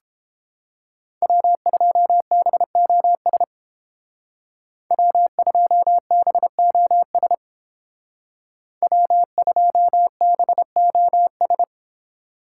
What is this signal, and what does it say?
Morse code, heard as a keyed tone, about 30 wpm, 700 Hz. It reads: W2BOS W2BOS W2BOS